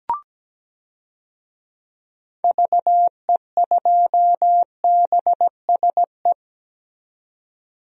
Morse code, heard as a keyed tone, about 17 wpm, 700 Hz. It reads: VE2BSE
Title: DXCC-100 Call Signs - 6 Characters - 17wpm